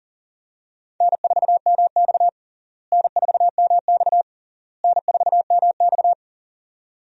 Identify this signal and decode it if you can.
Morse code, heard as a keyed tone, about 40 wpm, 700 Hz. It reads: N4MX N4MX N4MX
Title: USA Call Signs - 4 Characters - Repeated 3x - 40wpm